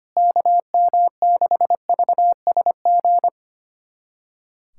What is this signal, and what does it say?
Morse code, heard as a keyed tone, about 25 wpm, 700 Hz. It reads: KM6VSG
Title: USA Callsign DE Callsign 25wpm